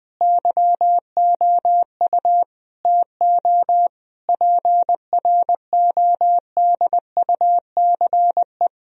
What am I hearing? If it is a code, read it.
Morse code, heard as a keyed tone, about 20 wpm, 700 Hz. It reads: YOU TO PRODUCE
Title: Sentences from Top 1000 Words - Review Part 3 - 20wpm